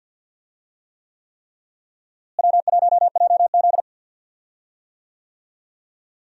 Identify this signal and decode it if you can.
Morse code, heard as a keyed tone, about 46 wpm, 700 Hz. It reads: W1JZ
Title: USA Call Signs - 4 Characters - Repeated 3x - 50wpm